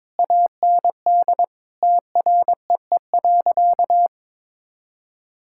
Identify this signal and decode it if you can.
Morse code, heard as a keyed tone, about 22 wpm, 700 Hz. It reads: AND TREE.
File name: Difficult Sentences with Hints - 22wpm